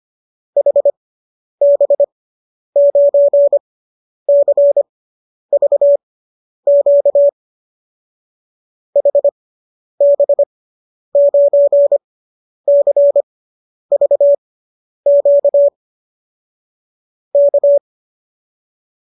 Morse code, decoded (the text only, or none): HB9CVQ HB9CVQ K